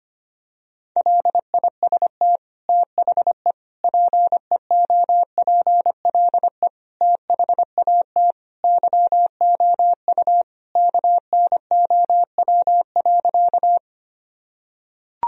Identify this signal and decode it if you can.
Morse code, heard as a keyed tone, about 25 wpm, 700 Hz. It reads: LIST THE PEOPLE THAT YOU KNOW.
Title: Sentences from Top 1000 Words - Review Part 2 - 25wpm